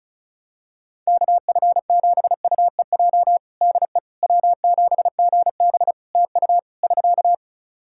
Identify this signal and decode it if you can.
Morse code, heard as a keyed tone, about 36 wpm, 700 Hz. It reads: KF7UEJ DE W7GB TU <SK>
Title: Practice QSO - Exchange 4 - Ending - 35wpm